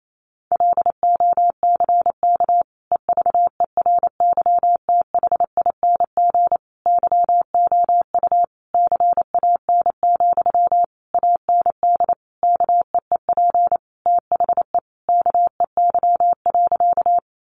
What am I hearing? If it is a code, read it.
Morse code, heard as a keyed tone, about 28 wpm, 700 Hz. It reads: LOCK EVERYTHING YOU CAN, AND KEEP THE KEY.
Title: Sentences from Top 1300 Words 28wpm